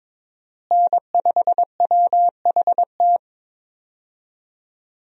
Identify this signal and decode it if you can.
Morse code, heard as a keyed tone, about 22 wpm, 700 Hz. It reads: N5WHT
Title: USA Call Signs - 5 Characters - Repeated 2x - 22wpm